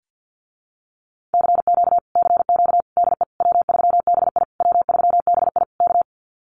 Morse code, heard as a keyed tone, about 49 wpm, 700 Hz. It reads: CQ CQ DE W3BI W3BI K
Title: Common QSO Phrases Encore 50wpm